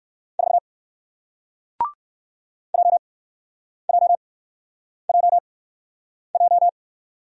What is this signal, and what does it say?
Morse code, heard as a keyed tone, about 58 wpm, spaced as for 9 wpm, 700 Hz. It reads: 4 JJJJ